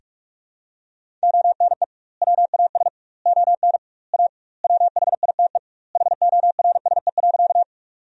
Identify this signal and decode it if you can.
Morse code, heard as a keyed tone, about 42 wpm, 700 Hz. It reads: ONE WAS ON A WHITE HORSE.
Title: 2x Word-Spacing - Sentences from Top 300 Words - 45wpm